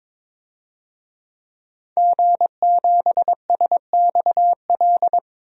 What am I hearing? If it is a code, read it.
Morse code, heard as a keyed tone, about 22 wpm, 700 Hz. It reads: G7SXL
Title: DXCC-100 Call Signs - 5 Characters - 22wpm